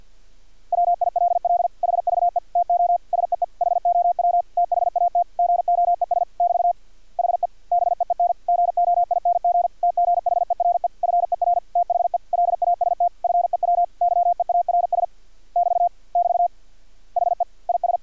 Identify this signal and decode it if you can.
Morse code, heard as a keyed tone, about 50 wpm, 700 Hz. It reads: WILL HVE TO SEE HOW THAT GOES <BT> HE BEEN GOING TGHERE FER THE LAST FEW YEARS <BT> <BT> HE IS
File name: archived / CRSnet